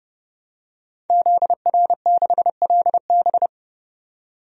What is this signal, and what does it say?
Morse code, heard as a keyed tone, about 30 wpm, 700 Hz. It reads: ZR6LB
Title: DXCC-100 Call Signs - 5 Characters - 30wpm